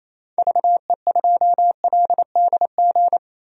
Morse code, heard as a keyed tone, about 27 wpm, 700 Hz. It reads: VE2LDG
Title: DXCC-100 Call Signs - 6 Characters - 28wpm